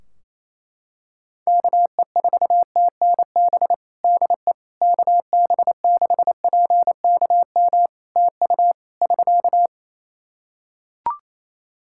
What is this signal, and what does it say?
Morse code, heard as a keyed tone, about 28 wpm, 700 Hz. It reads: KE4TNB DE KB6PKM TU <SK>
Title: Practice QSO - Exchange 4 - Ending - 28wpm